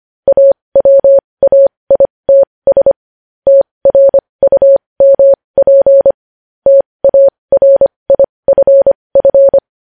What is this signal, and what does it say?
Morse code, heard as a keyed tone, about 25 wpm, spaced as for 20 wpm, 550 Hz. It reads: AWAITS TRUMP TARIFF
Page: News Headlines in Morse Code at 20 WPM